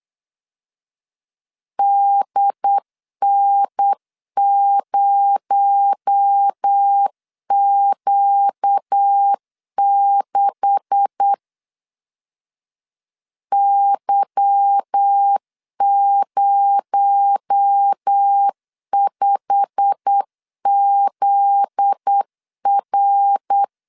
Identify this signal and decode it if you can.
Morse code, heard as a keyed tone, about 8 wpm, 800 Hz. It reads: DN0Q6 Y05ZR
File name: LEZIONI / 10 SIGNS / 10A SLASH 8wpm 3452